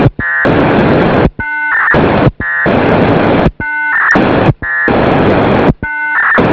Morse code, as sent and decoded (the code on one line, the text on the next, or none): none